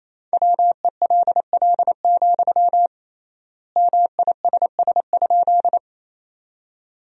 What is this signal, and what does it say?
Morse code, heard as a keyed tone, about 27 wpm, 700 Hz. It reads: WELL, MISS?
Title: 3x Word-Spacing - Sentences from Top 500 Words - 28wpm